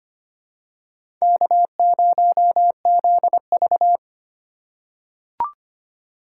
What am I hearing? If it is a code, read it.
Morse code, heard as a keyed tone, about 25 wpm, 700 Hz. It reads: K0ZV
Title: USA Call Signs - 4 Characters - 2x Character Spacing - 25wpm